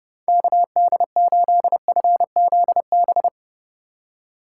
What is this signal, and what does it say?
Morse code, heard as a keyed tone, about 30 wpm, 700 Hz. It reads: KD8FZB
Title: USA Call Signs - 6 Characters - 4x Character Spacing - 30wpm